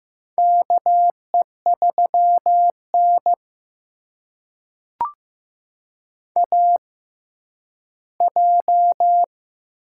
Morse code, heard as a keyed tone, about 15 wpm, spaced as for 10 wpm, 700 Hz. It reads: KE3N AJ
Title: USA Call Signs - 4 Characters - 6x Character Spacing - 15wpm